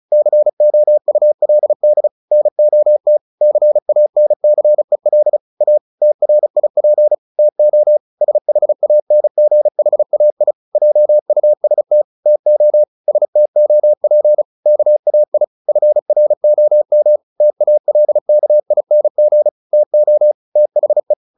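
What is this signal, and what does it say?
Morse code, heard as a keyed tone, about 34 wpm, 600 Hz. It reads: COULD NOT CANCEL A TRIP TO SHANGHAI JUST TO STOP KAI FROM TALKING TO THE
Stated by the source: Morse practice at 35 wpm from CW med Gnister, sending Ken Follett's novel Never in English.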